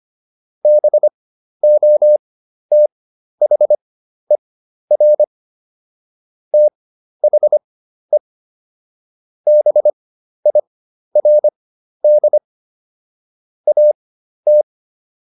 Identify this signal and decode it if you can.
Morse code, heard as a keyed tone, about 25 wpm, spaced as for 12 wpm, 600 Hz. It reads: BOTHER THE BIRD AT